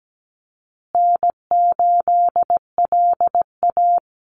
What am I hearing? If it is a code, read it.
Morse code, heard as a keyed tone, about 17 wpm, 700 Hz. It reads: N8LA
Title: Easy Call Signs Repeated 2x - 17wpm